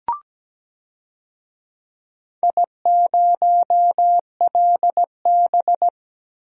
Morse code, heard as a keyed tone, about 17 wpm, 700 Hz. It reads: I0LB
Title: DXCC-100 Call Signs - 4 Characters - Repeated 3x - 17wpm